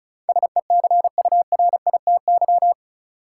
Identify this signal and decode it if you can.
Morse code, heard as a keyed tone, about 36 wpm, 700 Hz. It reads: SECURITY
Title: Sentences from Top 1000 Words - Review Part 4 - New Word Drill - 35wpm